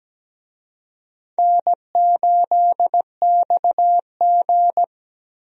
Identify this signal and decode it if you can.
Morse code, heard as a keyed tone, about 17 wpm, 700 Hz. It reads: N8XG
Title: USA Call Signs - 4 Characters - 17wpm